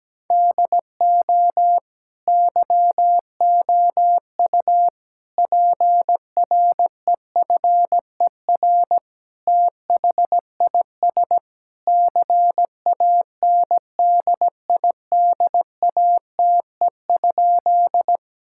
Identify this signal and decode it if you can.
Morse code, heard as a keyed tone, about 17 wpm, 700 Hz. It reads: DO YOU PREFER THIS CANDIDATE?
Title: Sentences from Top 1600 Words 17wpm